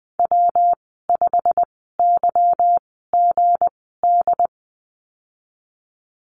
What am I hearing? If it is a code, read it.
Morse code, heard as a keyed tone, about 19 wpm, spaced as for 14 wpm, 700 Hz. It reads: W5YGD
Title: 179 Lesson 29g - Slash - Call Signs 20wpm